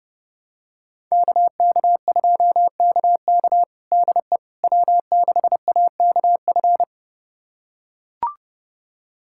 Morse code, heard as a keyed tone, about 30 wpm, 700 Hz. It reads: KK2KK DE W6AKF
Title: USA Callsign DE Callsign - 5 Characters - 30wpm